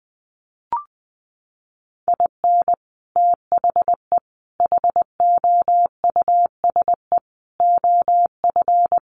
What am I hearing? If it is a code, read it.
Morse code, heard as a keyed tone, about 20 wpm, 700 Hz. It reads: IN THE HOUSE OF
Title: Sets of 5 Words Encore 20wpm